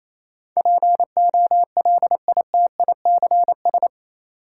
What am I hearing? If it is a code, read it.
Morse code, heard as a keyed tone, about 28 wpm, 700 Hz. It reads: POLITICS
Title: Sentences from Top 1000 Words - New Word Drill - 28wpm